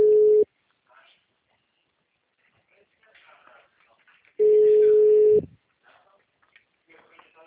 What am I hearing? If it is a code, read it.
Ringback tone (Europe, 425 Hz), with speech.
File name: out- -2023- - - .112